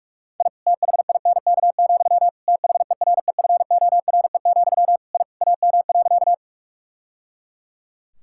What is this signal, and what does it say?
Morse code, heard as a keyed tone, about 47 wpm, 700 Hz. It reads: I THINK, THEREFORE, I AM.
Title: Sentences from Top 1000 Words - Review Part 5 - 45wpm